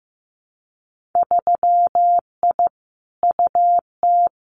Morse code, heard as a keyed tone, about 15 wpm, 700 Hz. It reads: 3I UT
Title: Winter Field Day Exchanges Encore 15wpm